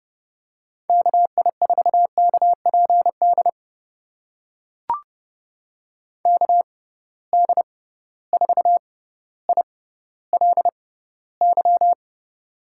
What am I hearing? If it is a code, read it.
Morse code, heard as a keyed tone, about 30 wpm, spaced as for 15 wpm, 700 Hz. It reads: KI4KPD KD4ILY
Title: USA Call Signs - 6 Characters - 6x Character Spacing - 30wpm